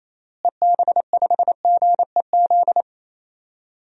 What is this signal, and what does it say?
Morse code, heard as a keyed tone, about 28 wpm, 700 Hz. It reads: EB5GEZ
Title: DXCC-100 Call Signs - 6 Characters - Repeated 2x - 28wpm